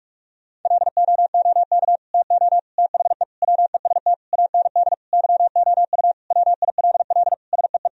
Morse code, heard as a keyed tone, about 46 wpm, 700 Hz. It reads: LOOK TO THE WEST AND YOU WILL SEE
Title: Sentences from Top 1000 Words - Review Part 2 - 45wpm